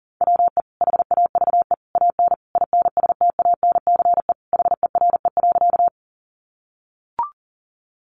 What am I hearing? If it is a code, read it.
Morse code, heard as a keyed tone, about 40 wpm, 700 Hz. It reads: WE HAVE AN INSTANCE HERE.
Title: Sentences from Top 800 Words 40wpm